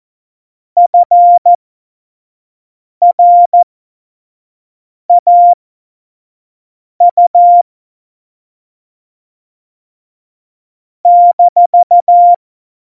Morse code, heard as a keyed tone, about 14 wpm, spaced as for 5 wpm, 700 Hz. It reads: FRAU -
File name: Morsekurs / Witze / Witz 02 Juni 2020, 25 F70